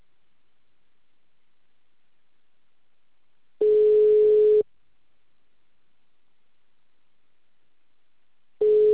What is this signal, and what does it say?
Ringback tone (Europe, 425 Hz).